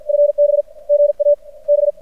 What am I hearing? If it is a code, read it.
Morse code, heard as a keyed tone, about 50 wpm, 590 Hz. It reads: FB GA B
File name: archived / CRSnet